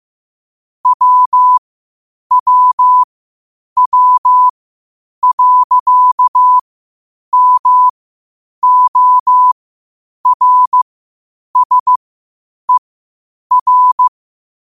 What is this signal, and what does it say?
Morse code, heard as a keyed tone, about 15 wpm, spaced as for 8 wpm, 1000 Hz. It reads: WWW.MORSER
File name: qotdfiles / 8wpm